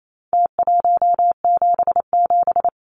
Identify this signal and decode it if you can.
Morse code, heard as a keyed tone, about 28 wpm, 700 Hz. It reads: T177
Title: Numbers - Contest Style - 4-Digit 28wpm